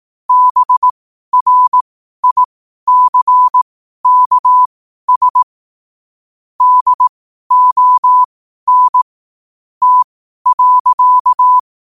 Morse code, heard as a keyed tone, about 18 wpm, spaced as for 12 wpm, 1000 Hz. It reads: BRICKS DON T.
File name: qotdfiles / 12wpm